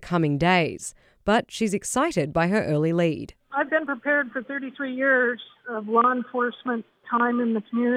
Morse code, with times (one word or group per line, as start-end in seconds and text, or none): none